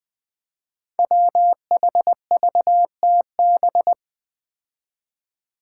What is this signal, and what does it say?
Morse code, heard as a keyed tone, about 20 wpm, 700 Hz. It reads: WHVTB
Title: Five Challenging Letter-Number Combinations 20wpm